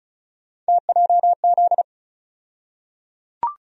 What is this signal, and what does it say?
Morse code, heard as a keyed tone, about 34 wpm, 700 Hz. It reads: TJZ
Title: USA Call Sign Postfixes - 3 Letters 35wpm